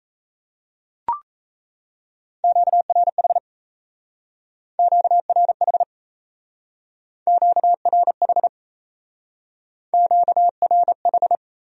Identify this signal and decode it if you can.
Morse code, heard as a keyed tone, about 34 wpm, 700 Hz. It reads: QRH QRH QRH QRH
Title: Speed-Racing Q-Codes - 28wpm